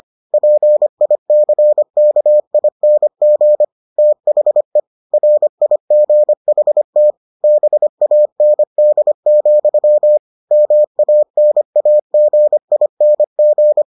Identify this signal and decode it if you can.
Morse code, heard as a keyed tone, about 25 wpm, 600 Hz. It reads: PICKING THE RIGHT BAND, MANAGING